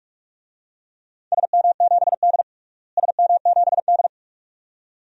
Morse code, heard as a keyed tone, about 46 wpm, 700 Hz. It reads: SM7D SM7D
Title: DXCC-100 Call Signs - 4 Characters - Repeated 2x - 45wpm